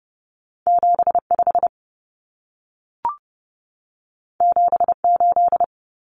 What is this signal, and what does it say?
Morse code, heard as a keyed tone, about 30 wpm, 700 Hz. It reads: 75 78
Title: Common Cut Numbers - 2-Digit 30wpm